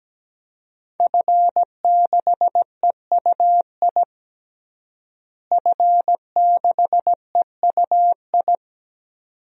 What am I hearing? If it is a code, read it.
Morse code, heard as a keyed tone, about 17 wpm, 700 Hz. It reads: F6EUI F6EUI
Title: DXCC-100 Call Signs - 5 Characters - Repeated 2x - 17wpm